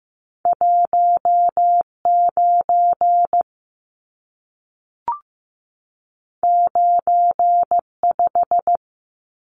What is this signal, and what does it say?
Morse code, heard as a keyed tone, about 15 wpm, 700 Hz. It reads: 19 95
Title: Numbers - 2-Digit 15wpm